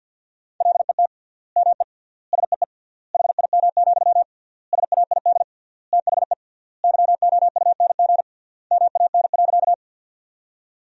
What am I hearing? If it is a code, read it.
Morse code, heard as a keyed tone, about 50 wpm, 700 Hz. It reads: LET ME SEE HIM, SAID THE YOUNG MAN.
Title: 3x Word-Spacing - Sentences from Top 400 Words - 50wpm